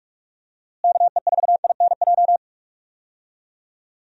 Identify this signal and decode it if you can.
Morse code, heard as a keyed tone, about 46 wpm, 700 Hz. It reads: KE4INJ
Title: USA Call Signs - 6 Characters - Repeated 3x - 45wpm